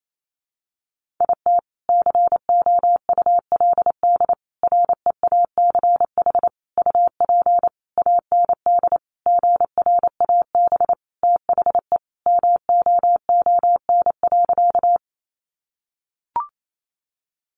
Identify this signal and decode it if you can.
Morse code, heard as a keyed tone, about 28 wpm, 700 Hz. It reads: IT COULD REACH UP AND GRAB THE MOON.
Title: A Sound Of Thunder 28wpm